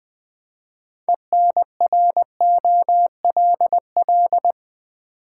Morse code, heard as a keyed tone, about 20 wpm, 690 Hz. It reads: ENROLL
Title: Build A Word - 2 Letters At A Time - 6-Letter Words - 20wpm